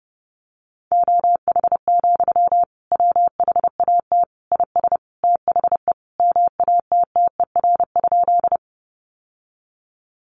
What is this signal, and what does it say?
Morse code, heard as a keyed tone, about 30 wpm, 700 Hz. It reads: OH, WHAT IS THE MATTER?
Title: Sentences from Top 700 Words 30wpm